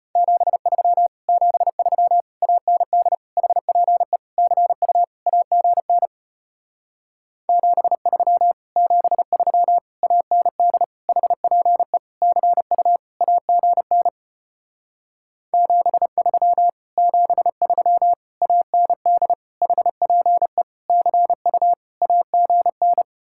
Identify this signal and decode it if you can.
Morse code, heard as a keyed tone, about 33 wpm, 700 Hz. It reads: 73 73 AND HPE CU AGN 73 73 AND HPE CU AGN 73 73 AND HPE CU AGN